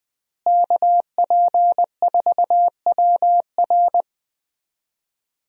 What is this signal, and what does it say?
Morse code, heard as a keyed tone, about 20 wpm, 700 Hz. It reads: KP4WR
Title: USA Call Signs - 5 Characters - Repeated 2x - 20wpm